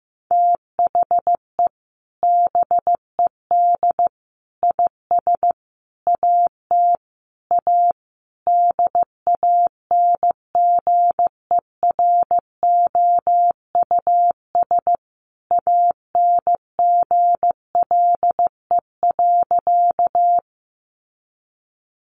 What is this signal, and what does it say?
Morse code, heard as a keyed tone, about 15 wpm, 700 Hz. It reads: THE BED IS AT A DANGEROUS ANGLE.
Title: Sentences from Top 1500 Words 15wpm